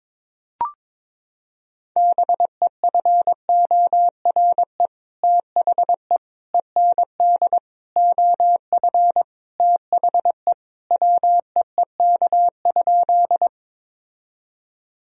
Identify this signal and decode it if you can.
Morse code, heard as a keyed tone, about 22 wpm, 700 Hz. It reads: BEFORE THE END OF THE WEEK?
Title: Sentences from Top 1000 Words - Review Part 2 - 22wpm